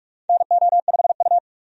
Morse code, heard as a keyed tone, about 42 wpm, 700 Hz. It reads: NO5U
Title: USA Call Signs - 4 Characters - 45wpm